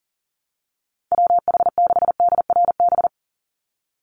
Morse code, heard as a keyed tone, about 40 wpm, 700 Hz. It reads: WH6DRB
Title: USA Call Signs 40wpm